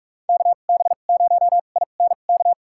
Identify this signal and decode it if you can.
Morse code, heard as a keyed tone, about 44 wpm, spaced as for 32 wpm, 700 Hz. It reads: KB0INK
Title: USA Call Signs - 6 Characters - 2x Character Spacing - 45wpm